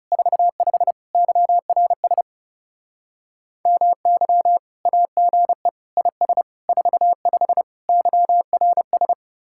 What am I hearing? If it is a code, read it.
Morse code, heard as a keyed tone, about 36 wpm, 700 Hz. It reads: 45 YRS MY AGE IS 45 YRS
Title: Speed-Racing Common QSO Phrases - 30wpm